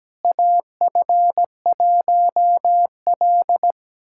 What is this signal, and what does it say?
Morse code, heard as a keyed tone, about 17 wpm, 700 Hz. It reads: AF1L
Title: USA Callsign DE Callsign - 4 Characters - 17wpm